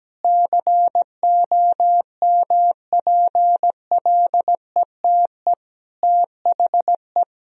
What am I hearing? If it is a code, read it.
Morse code, heard as a keyed tone, about 17 wpm, 700 Hz. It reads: COMPLETE THE
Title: Sentences from Top 1000 Words - Review Part 1 - 17wpm